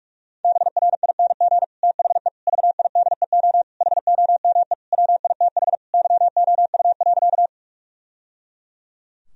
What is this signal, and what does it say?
Morse code, heard as a keyed tone, about 46 wpm, 700 Hz. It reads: BRING THE VIDEO HOME WITH YOU.
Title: Sentences from Top 1000 Words - Review Part 4 - 45wpm